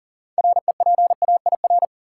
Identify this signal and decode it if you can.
Morse code, heard as a keyed tone, about 38 wpm, 700 Hz. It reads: REPAIR
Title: Six Letter Words - 40wpm